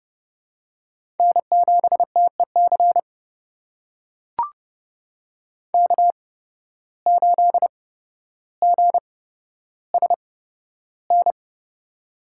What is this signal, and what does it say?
Morse code, heard as a keyed tone, about 30 wpm, spaced as for 13 wpm, 700 Hz. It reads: N7TEC K8GSN
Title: USA Call Signs - 5 Characters - 8x Character Spacing - 30wpm